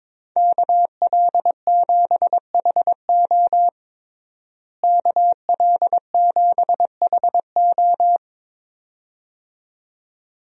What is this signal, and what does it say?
Morse code, heard as a keyed tone, about 22 wpm, 700 Hz. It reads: KL7HO KL7HO
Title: DXCC-100 Call Signs - 5 Characters - Repeated 2x - 22wpm